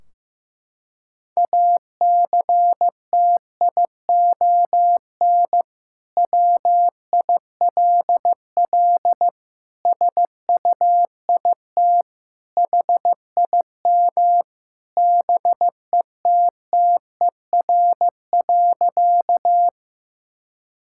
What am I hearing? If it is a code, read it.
Morse code, heard as a keyed tone, about 16 wpm, 700 Hz. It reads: ACTION WILL SUIT HIM BETTER.
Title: Sentences from Top 1600 Words 15wpm